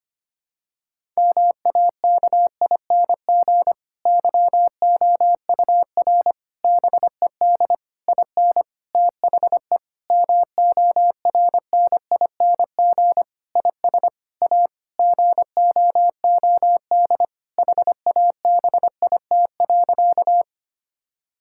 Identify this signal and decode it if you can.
Morse code, heard as a keyed tone, about 25 wpm, 700 Hz. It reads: MAKING YOUR BED IN THE MORNING IS A GOOD HABIT.
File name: practice / 25wpm / Difficult Sentences with Hints - 25wpm